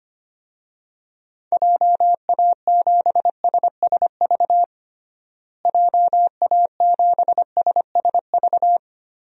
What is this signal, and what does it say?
Morse code, heard as a keyed tone, about 25 wpm, 700 Hz. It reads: JA7SSV JA7SSV
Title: DXCC-100 Call Signs - 6 Characters - Repeated 3x - 25wpm